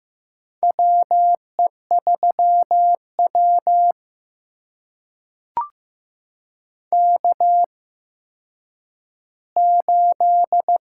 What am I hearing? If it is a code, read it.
Morse code, heard as a keyed tone, about 15 wpm, spaced as for 9 wpm, 700 Hz. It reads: WE3W K8
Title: USA Call Signs - 4 Characters - 8x Character Spacing - 15wpm